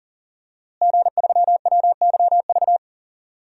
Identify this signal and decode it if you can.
Morse code, heard as a keyed tone, about 40 wpm, 700 Hz. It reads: G3WYV
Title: DXCC-100 Call Signs - 5 Characters - Repeated 3x - 40wpm